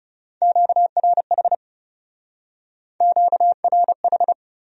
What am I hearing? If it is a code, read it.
Morse code, heard as a keyed tone, about 35 wpm, 700 Hz. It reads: QRH QRH
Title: Speed-Racing Q-Codes - 30wpm